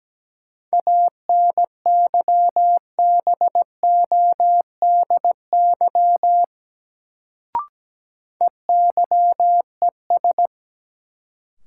Morse code, heard as a keyed tone, about 17 wpm, 700 Hz. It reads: ANYBODY EYES
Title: Sets of 5 Words - New Word Drill - 17wpm